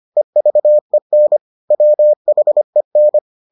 Morse code, heard as a keyed tone, about 25 wpm, 600 Hz. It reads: EVEN WHEN